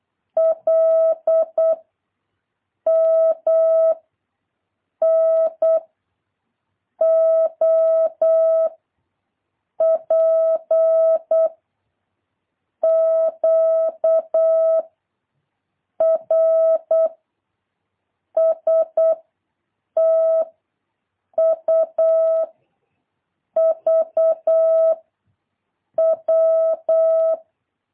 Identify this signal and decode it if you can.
Morse code, heard as a keyed tone, about 8 wpm, 650 Hz. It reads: L M N O P Q R S T U V W
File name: jh8cbh / 2023cw